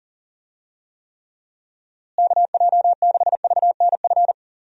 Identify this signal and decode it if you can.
Morse code, heard as a keyed tone, about 37 wpm, 700 Hz. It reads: KJ6VNF
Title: USA Call Signs - 6 Characters - Repeated 3x - 40wpm